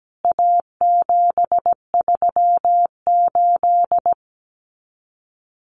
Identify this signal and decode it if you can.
Morse code, heard as a keyed tone, about 17 wpm, 700 Hz. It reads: A738
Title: CWT Exchange 17wpm